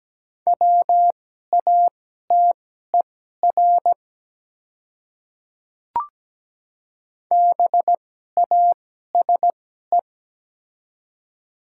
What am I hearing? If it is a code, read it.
Morse code, heard as a keyed tone, about 17 wpm, spaced as for 12 wpm, 700 Hz. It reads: WATER BASE
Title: 2x Character Spacing - Top 300 Words - 17wpm